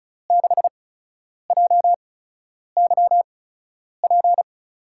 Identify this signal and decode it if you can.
Morse code, heard as a keyed tone, about 36 wpm, spaced as for 10 wpm, 700 Hz. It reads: 6JYP
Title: USA Call Signs - 6 Characters - 8x Character Spacing - 35wpm